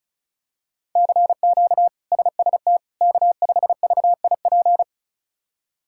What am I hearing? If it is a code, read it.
Morse code, heard as a keyed tone, about 34 wpm, 700 Hz. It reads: CQ SST K5VIP
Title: K1USN SST QSO Practice - 35wpm